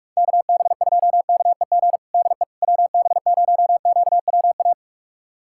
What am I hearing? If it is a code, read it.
Morse code, heard as a keyed tone, about 43 wpm, 700 Hz. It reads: KB2KEG DE WB0QWA
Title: USA Callsign DE Callsign 45wpm